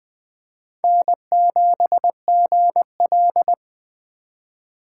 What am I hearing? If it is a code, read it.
Morse code, heard as a keyed tone, about 19 wpm, 700 Hz. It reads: N7GL